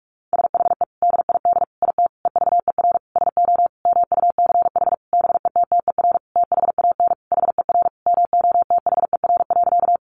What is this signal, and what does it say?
Morse code, heard as a keyed tone, about 46 wpm, 700 Hz. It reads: SHE DID IT EVER SO MUCH BETTER THAN HER MOTHER.
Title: Sentences from Top 400 Words 45wpm